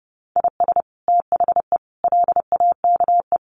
Morse code, heard as a keyed tone, about 30 wpm, 700 Hz. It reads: IS THE LAKE
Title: Sentences from Top 1400 Words 30wpm